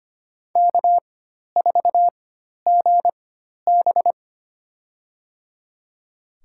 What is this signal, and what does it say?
Morse code, heard as a keyed tone, about 25 wpm, spaced as for 12 wpm, 700 Hz. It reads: K4GB
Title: USA Call Signs - 4x Character Spacing - 25wpm